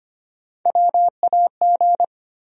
Morse code, heard as a keyed tone, about 25 wpm, 700 Hz. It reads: WAG
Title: USA Call Sign Postfixes - 3 Letters 25wpm